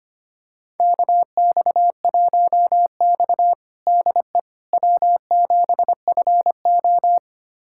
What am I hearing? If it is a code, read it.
Morse code, heard as a keyed tone, about 25 wpm, 700 Hz. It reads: KX1X DE W7FO
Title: USA Callsign DE Callsign - 4 Characters - 25wpm